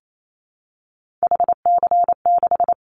Morse code, heard as a keyed tone, about 28 wpm, 700 Hz. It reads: HC6
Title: Callsign Prefixes for Top 100 DXCC - 28wpm